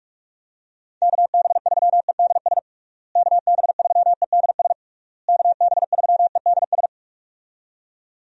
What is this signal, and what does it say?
Morse code, heard as a keyed tone, about 46 wpm, 700 Hz. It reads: KB3EDS KB3EDS KB3EDS
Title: USA Call Signs - 6 Characters - Repeated 3x - 45wpm